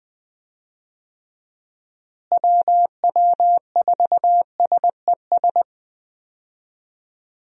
Morse code, heard as a keyed tone, about 20 wpm, 700 Hz. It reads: WW4SES
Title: USA Call Signs - 6 Characters - Repeated 2x - 20wpm